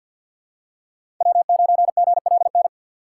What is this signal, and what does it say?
Morse code, heard as a keyed tone, about 53 wpm, 700 Hz. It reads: W9GLN
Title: USA Call Signs - 5 Characters - 4x Character Spacing - 50wpm